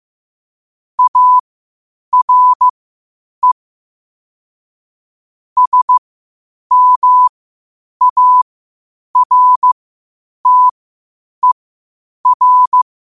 Morse code, heard as a keyed tone, about 15 wpm, spaced as for 8 wpm, 1000 Hz. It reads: ARE SMARTER
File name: qotdfiles / 8wpm